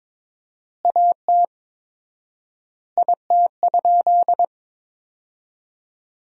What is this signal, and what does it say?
Morse code, heard as a keyed tone, about 21 wpm, 700 Hz. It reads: AT IT?
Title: 4x Word-Spacing - Sentences from Top 100 Words - 22wpm